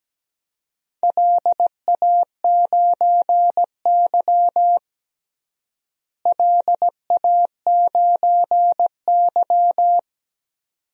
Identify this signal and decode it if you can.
Morse code, heard as a keyed tone, about 17 wpm, 700 Hz. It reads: LA9Y LA9Y
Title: DXCC-100 Call Signs - 4 Characters - Repeated 2x - 17wpm